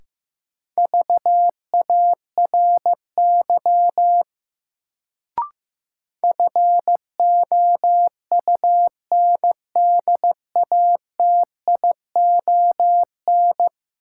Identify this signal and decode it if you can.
Morse code, heard as a keyed tone, about 15 wpm, 700 Hz. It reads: VARY FOUNDATION
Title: Sentences from Top 1200 Words - New Word Drill - 15wpm